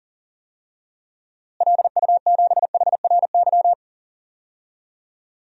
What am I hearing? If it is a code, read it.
Morse code, heard as a keyed tone, about 40 wpm, 700 Hz. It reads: LU7HRY
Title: DXCC-100 Call Signs - 6 Characters - Repeated 3x - 40wpm